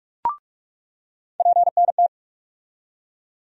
Morse code, heard as a keyed tone, about 46 wpm, 700 Hz. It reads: PNT
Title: USA Call Sign Postfixes - 3 Letters 45wpm